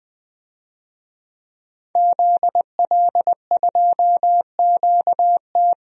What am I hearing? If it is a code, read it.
Morse code, heard as a keyed tone, about 20 wpm, 700 Hz. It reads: ZL2QT
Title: DXCC-100 Call Signs - 5 Characters - Repeated 3x - 20wpm